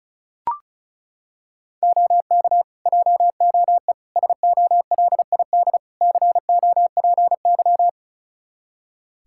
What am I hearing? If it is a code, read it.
Morse code, heard as a keyed tone, about 34 wpm, 700 Hz. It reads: OK JOE SOLID COPY
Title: Practice QSO - Exchange 3 - More Info - 35wpm